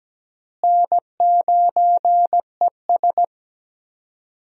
Morse code, heard as a keyed tone, about 17 wpm, 700 Hz. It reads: N9ES
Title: USA Call Signs - 4 Characters - Repeated 3x - 17wpm